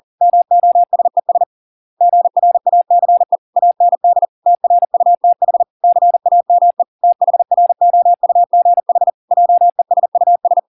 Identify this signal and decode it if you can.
Morse code, heard as a keyed tone, about 40 wpm, 700 Hz. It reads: MOSES GRACE AND TRUTH CAME THROUGH JESUS